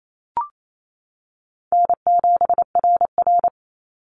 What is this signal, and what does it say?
Morse code, heard as a keyed tone, about 28 wpm, 700 Hz. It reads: N7RR
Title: NA SOTA Activator Callsigns 28wpm